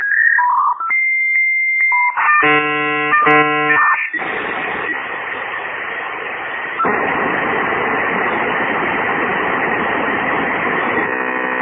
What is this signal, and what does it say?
Dial-up modem handshake.